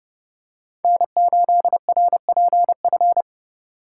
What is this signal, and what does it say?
Morse code, heard as a keyed tone, about 28 wpm, 700 Hz. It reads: N8RPF